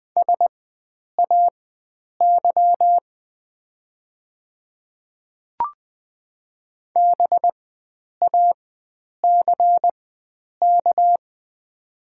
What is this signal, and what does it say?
Morse code, heard as a keyed tone, about 20 wpm, spaced as for 9 wpm, 700 Hz. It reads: SAY BACK
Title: 4x Character Spacing - Top 300 Words - 20wpm